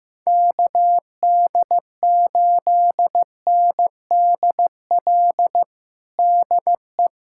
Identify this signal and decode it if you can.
Morse code, heard as a keyed tone, about 15 wpm, 700 Hz. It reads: KD8NDL DE
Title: USA Callsign DE Callsign 15wpm